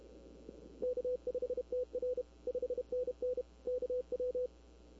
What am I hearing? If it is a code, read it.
Morse code, heard as a keyed tone, about 31 wpm, 500 Hz. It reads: K5TR 5NN KW
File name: CW / 2005-k5tr arrl cw dx -20:00